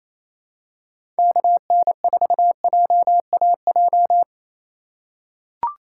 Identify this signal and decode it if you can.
Morse code, heard as a keyed tone, about 27 wpm, 700 Hz. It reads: KN4JAJ
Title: USA Call Signs - 6 Characters - 8x Character Spacing - 28wpm